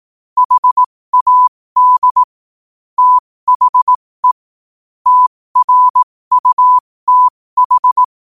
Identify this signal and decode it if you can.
Morse code, heard as a keyed tone, about 18 wpm, spaced as for 15 wpm, 1000 Hz. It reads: HAD THE TRUTH
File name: qotdfiles / 15wpm